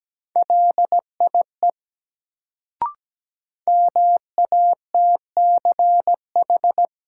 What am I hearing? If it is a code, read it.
Morse code, heard as a keyed tone, about 17 wpm, 700 Hz. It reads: LIE MATCH
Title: Sentences from Top 1600 Words - New Word Drill Alt - 17wpm